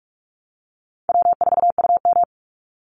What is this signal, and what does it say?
Morse code, heard as a keyed tone, about 44 wpm, 700 Hz. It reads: W4UM
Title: Easy Call Signs Repeated 2x - 45wpm